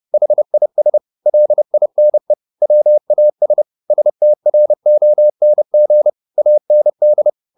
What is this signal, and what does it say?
Morse code, heard as a keyed tone, about 30 wpm, 600 Hz. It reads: HIS LINE WAS STRONG AND
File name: the old man and the sea / the old man and the sea 1 30wpm